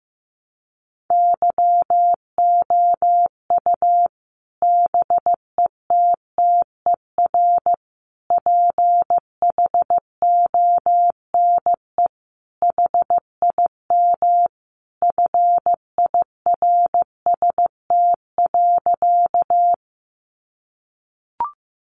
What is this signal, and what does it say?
Morse code, heard as a keyed tone, about 15 wpm, 700 Hz. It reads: YOU BETTER PHONE HIM FIRST.
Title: Sentences from Top 1100 Words 15wpm